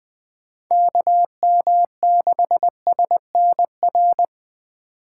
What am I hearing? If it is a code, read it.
Morse code, heard as a keyed tone, about 20 wpm, 700 Hz. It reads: KM6SNR
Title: Speed Racing - USA 2x3 Call Signs 17wpm